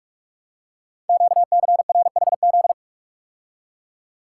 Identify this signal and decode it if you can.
Morse code, heard as a keyed tone, about 44 wpm, 700 Hz. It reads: QCRHZ
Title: Code Groups - 5 Letters - 45wpm